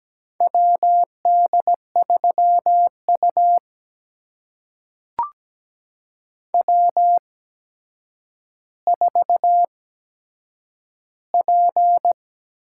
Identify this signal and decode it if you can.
Morse code, heard as a keyed tone, about 17 wpm, spaced as for 8 wpm, 700 Hz. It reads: WD3U W4P
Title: USA Call Signs - 4 Characters - 8x Character Spacing - 17wpm